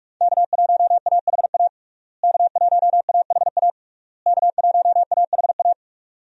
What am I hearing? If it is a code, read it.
Morse code, heard as a keyed tone, about 46 wpm, 700 Hz. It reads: K1AHA K1AHA K1AHA
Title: USA Call Signs - 5 Characters - Repeated 3x - 45wpm